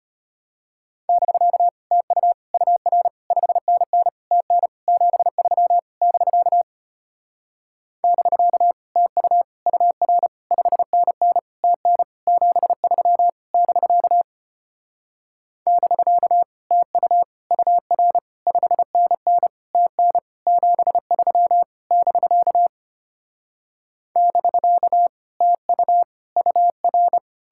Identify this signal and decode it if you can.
Morse code, heard as a keyed tone, about 33 wpm, 700 Hz. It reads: <BK> TU UR 5NN TN 73 <BK> <BK> TU UR 5NN TN 73 <BK> <BK> TU UR 5NN TN 73 <BK> <BK> TU UR